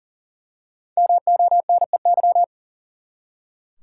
Morse code, heard as a keyed tone, about 39 wpm, 700 Hz. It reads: MONEY
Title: Five Letter Words - 40wpm